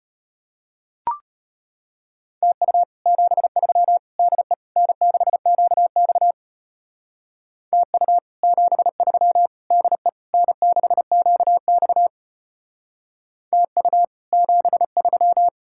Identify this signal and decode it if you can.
Morse code, heard as a keyed tone, about 31 wpm, 700 Hz. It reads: TU 73 DE N6QX TU 73 DE N6QX TU 73
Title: Speed-Racing Common QSO Phrases - 25wpm